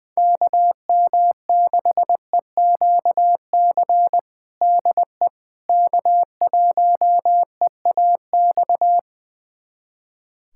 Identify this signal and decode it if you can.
Morse code, heard as a keyed tone, about 20 wpm, 700 Hz. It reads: KM6EQC DE K1EAX
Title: USA Callsign DE Callsign 20wpm